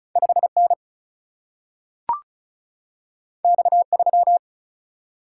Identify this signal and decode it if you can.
Morse code, heard as a keyed tone, about 36 wpm, 700 Hz. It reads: 5 N X 3
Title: Single-Letter Single-Number 35wpm